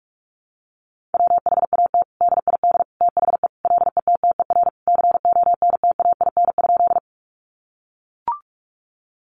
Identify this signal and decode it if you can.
Morse code, heard as a keyed tone, about 44 wpm, 700 Hz. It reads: WHAT DID THE LETTER CONTAIN?
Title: Sentences from Top 500 Words 45wpm